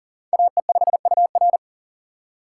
Morse code, heard as a keyed tone, about 40 wpm, 690 Hz. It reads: AE5UR
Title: USA Call Signs - 5 Characters - 6x Character Spacing - 40wpm